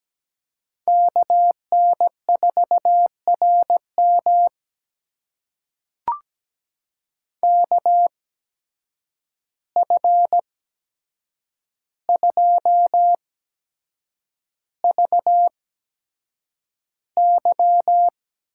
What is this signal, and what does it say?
Morse code, heard as a keyed tone, about 17 wpm, spaced as for 7 wpm, 700 Hz. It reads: KN4RM KF2VY
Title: USA Call Signs - 8x Character Spacing - 17wpm